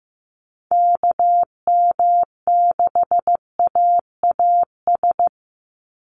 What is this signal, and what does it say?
Morse code, heard as a keyed tone, about 15 wpm, 700 Hz. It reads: KM6AAS
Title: USA Call Signs 15wpm